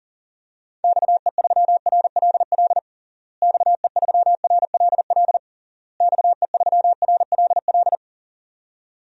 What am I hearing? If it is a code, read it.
Morse code, heard as a keyed tone, about 40 wpm, 700 Hz. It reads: XE3RLL XE3RLL XE3RLL
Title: DXCC-100 Call Signs - 6 Characters - Repeated 3x - 40wpm